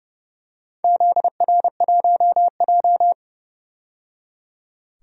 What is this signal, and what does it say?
Morse code, heard as a keyed tone, about 30 wpm, 700 Hz. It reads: ZR1J
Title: DXCC-100 Call Signs - 4 Characters - 30wpm